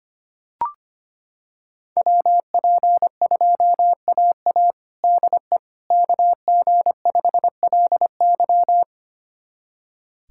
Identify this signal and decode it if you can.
Morse code, heard as a keyed tone, about 25 wpm, 700 Hz. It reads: WP2AA DE KG5LY
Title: USA Callsign DE Callsign - 5 Characters - 25wpm